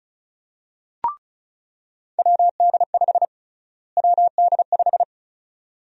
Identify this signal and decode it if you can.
Morse code, heard as a keyed tone, about 36 wpm, 700 Hz. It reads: WD5 WD5
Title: USA Call Sign Prefixes - Repeated 2x - 35wpm